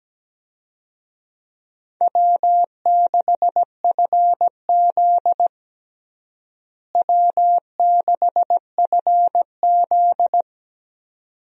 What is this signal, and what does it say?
Morse code, heard as a keyed tone, about 17 wpm, 700 Hz. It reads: W6FZ W6FZ
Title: USA Call Signs - 4 Characters - Repeated 2x - 17wpm